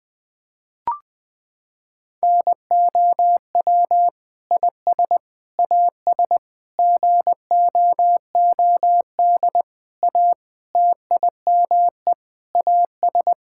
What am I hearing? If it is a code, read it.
Morse code, heard as a keyed tone, about 21 wpm, 700 Hz. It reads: NOW IS AS GOOD A TIME AS
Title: Sentences with 8-Word Phrases - 20wpm